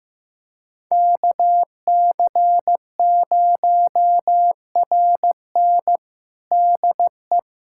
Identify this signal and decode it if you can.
Morse code, heard as a keyed tone, about 15 wpm, 700 Hz. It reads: KC0RN DE
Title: USA Callsign DE Callsign 15wpm